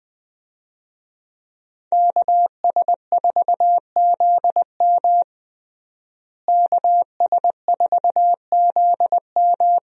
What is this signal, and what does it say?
Morse code, heard as a keyed tone, about 20 wpm, 700 Hz. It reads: KS4ZM KS4ZM
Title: USA Call Signs - 5 Characters - Repeated 2x - 20wpm